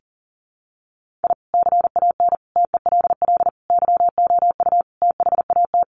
Morse code, heard as a keyed tone, about 40 wpm, 700 Hz. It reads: I CAN TELL YOU THAT
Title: Sets of 5 Words 40wpm